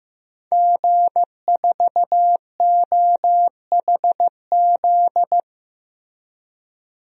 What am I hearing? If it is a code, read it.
Morse code, heard as a keyed tone, about 15 wpm, 700 Hz. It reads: G4OHZ
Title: DXCC-100 Call Signs - 5 Characters - Repeated 3x - 15wpm